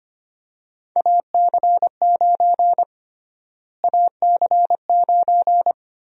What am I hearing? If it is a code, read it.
Morse code, heard as a keyed tone, about 25 wpm, 700 Hz. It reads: AC9 AC9
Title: USA Call Sign Prefixes - Repeated 2x - 25wpm